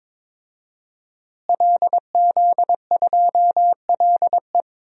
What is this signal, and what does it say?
Morse code, heard as a keyed tone, about 22 wpm, 700 Hz. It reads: LZ2LE
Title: DXCC-100 Call Signs - 5 Characters - Repeated 2x - 22wpm